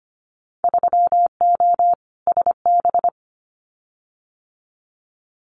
Morse code, heard as a keyed tone, about 25 wpm, 700 Hz. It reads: 3O SB
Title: Winter Field Day Exchanges Encore 25wpm